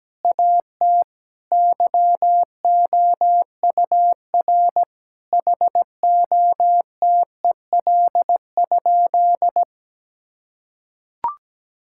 Morse code, heard as a keyed tone, about 17 wpm, 700 Hz. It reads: AT YOUR HOTEL?
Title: Sentences from Top 1600 Words 17wpm